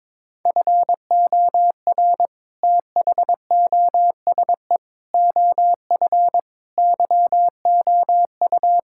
Morse code, heard as a keyed tone, about 22 wpm, 700 Hz. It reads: FOR THOSE OF YOU
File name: practice / 22wpm / Sets of 7 Words - 22wpm